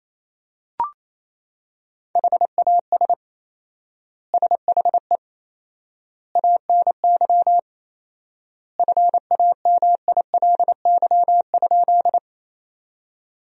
Morse code, heard as a keyed tone, about 27 wpm, 700 Hz. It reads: HAS SHE ANY FAMILY?
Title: 4x Word-Spacing - Sentences from Top 400 Words - 28wpm